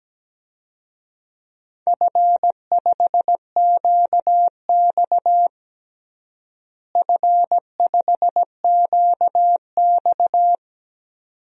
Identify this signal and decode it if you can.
Morse code, heard as a keyed tone, about 17 wpm, 700 Hz. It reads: F5QX F5QX
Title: DXCC-100 Call Signs - 4 Characters - Repeated 2x - 17wpm